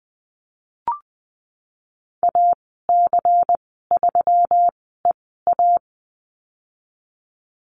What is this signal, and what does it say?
Morse code, heard as a keyed tone, about 20 wpm, spaced as for 15 wpm, 700 Hz. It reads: AC3EA
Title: 122 Lesson 20g - Number 3 - Call Signs 20wpm